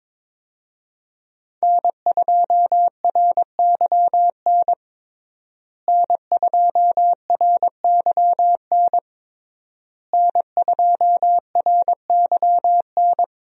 Morse code, heard as a keyed tone, about 22 wpm, 700 Hz. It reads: N2RYN N2RYN N2RYN
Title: USA Call Signs - 5 Characters - Repeated 3x - 22wpm